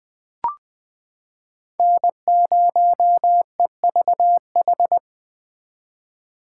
Morse code, heard as a keyed tone, about 21 wpm, 700 Hz. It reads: N0EVH
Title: 2023 SOTA Chaser Call Signs - 20wpm